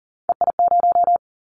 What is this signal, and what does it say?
Morse code, heard as a keyed tone, about 37 wpm, 700 Hz. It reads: EI0
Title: Callsign Prefixes for Top 50 DXCC - 40wpm